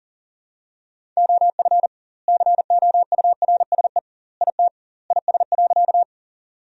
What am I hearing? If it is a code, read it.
Morse code, heard as a keyed tone, about 40 wpm, 700 Hz. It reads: OF COURSE IT IS.
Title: 2x Word-Spacing - Sentences from Top 500 Words - 40wpm